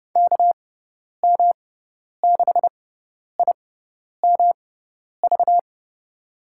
Morse code, heard as a keyed tone, about 30 wpm, spaced as for 10 wpm, 700 Hz. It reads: KM6IMV
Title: USA Call Signs - 6 Characters - 6x Character Spacing - 30wpm